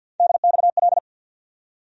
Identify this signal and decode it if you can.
Morse code, heard as a keyed tone, about 50 wpm, 700 Hz. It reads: DXL
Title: USA Call Sign Postfixes - 3 Letters 50wpm